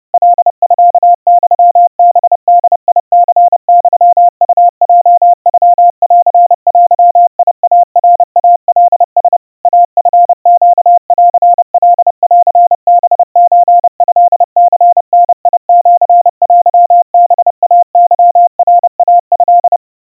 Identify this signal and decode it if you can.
Morse code (Japanese Wabun code), heard as a keyed tone, about 30 wpm, 700 Hz. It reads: カミユハボニユウヲノンデイナイカラ イチネンカンハソトニダシテハイケナイト
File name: 110 cpm 700 Hz Crispy